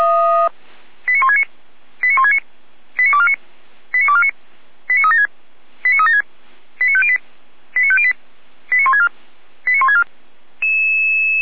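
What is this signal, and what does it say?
Five-tone selective call (ZVEI), heard as quick runs of five tones, tones of about 80 ms. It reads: 89169 89169 89249 89249 89376 89376 89508 89508 89164 89164
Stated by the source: ZVEI five-tone selective call 89167, received by SDR, fire brigade Orlamünde (feuerwehr-orlamuende).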